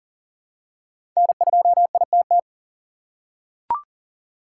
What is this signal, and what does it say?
Morse code, heard as a keyed tone, about 40 wpm, 700 Hz. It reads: N2ITT
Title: USA Call Signs - 5 Characters - 2x Character Spacing - 40wpm